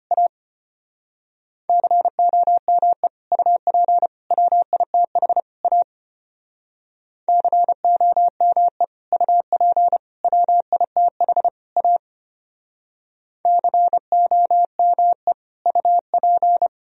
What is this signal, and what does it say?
Morse code, heard as a keyed tone, about 29 wpm, 700 Hz. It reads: A COME UP WITH A COME UP WITH A COME UP
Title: Speed-Racing - Sets of 4 Words 25wpm